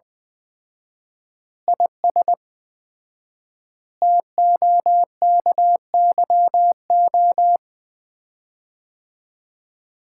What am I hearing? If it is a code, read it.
Morse code, heard as a keyed tone, about 21 wpm, 700 Hz. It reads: IS TOKYO
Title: Common QSO Phrases - 4x Word-Spacing - 20wpm